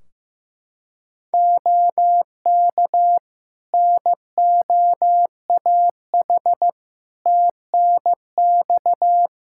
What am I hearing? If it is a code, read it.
Morse code, heard as a keyed tone, about 15 wpm, 700 Hz. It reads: OK NOAH TNX
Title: Practice QSO - Exchange 4 - Ending - 15wpm